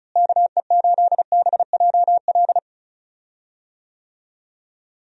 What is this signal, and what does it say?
Morse code, heard as a keyed tone, about 34 wpm, 700 Hz. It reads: KE8BJL
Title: USA Call Signs - 6 Characters - Repeated 2x - 35wpm